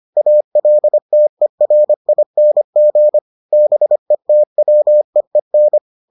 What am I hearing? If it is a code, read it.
Morse code, heard as a keyed tone, about 25 wpm, 600 Hz. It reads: ALTERING BETWEEN